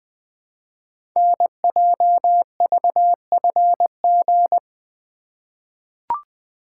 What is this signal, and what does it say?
Morse code, heard as a keyed tone, about 20 wpm, 700 Hz. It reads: NJVFG
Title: Code Groups - 5 Letters - 20wpm